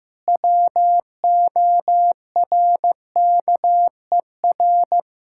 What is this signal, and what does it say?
Morse code, heard as a keyed tone, about 15 wpm, 700 Hz. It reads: WORKER
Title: K1USN SST QSO Practice 15wpm